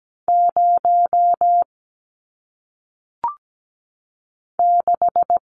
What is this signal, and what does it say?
Morse code, heard as a keyed tone, about 17 wpm, 700 Hz. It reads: 0 6
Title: Numbers - 1-Digit 17wpm